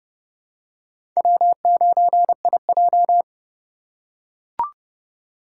Morse code, heard as a keyed tone, about 30 wpm, 700 Hz. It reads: W9IJ
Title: USA Call Signs - 4 Characters - 4x Character Spacing - 30wpm